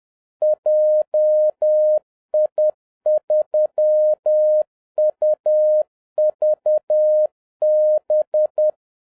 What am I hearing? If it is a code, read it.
Morse code, heard as a keyed tone, about 10 wpm, 610 Hz. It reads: JI3UVB